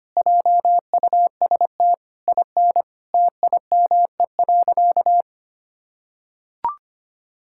Morse code, heard as a keyed tone, about 25 wpm, 700 Hz. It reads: JUST IN TIME.
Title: Sentences from Top 1600 Words 25wpm